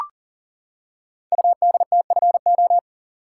Morse code, heard as a keyed tone, about 40 wpm, 700 Hz. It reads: UDTFO
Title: Code Groups - 5 Letters - 40wpm